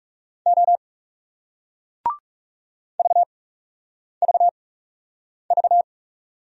Morse code, heard as a keyed tone, about 37 wpm, spaced as for 8 wpm, 700 Hz. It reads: O VVV
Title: Speed-Racing Single Letters 30wpm